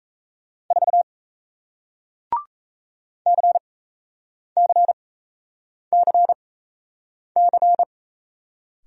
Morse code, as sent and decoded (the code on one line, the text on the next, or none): ....- / -.-. -.-. -.-. -.-.
4 CCCC